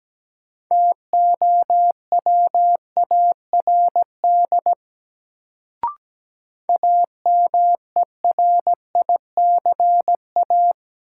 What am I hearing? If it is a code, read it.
Morse code, heard as a keyed tone, about 17 wpm, 700 Hz. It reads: TOWARD AMERICA
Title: Sets of 5 Words - New Word Drill - 17wpm